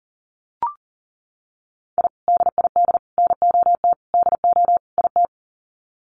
Morse code, heard as a keyed tone, about 37 wpm, 700 Hz. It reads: I DID NOT DO IT
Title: Sets of 5 Words Encore2 40wpm